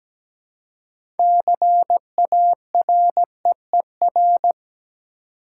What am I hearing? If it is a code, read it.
Morse code, heard as a keyed tone, about 17 wpm, 700 Hz. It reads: CAREER
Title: Six Letter Words - 17wpm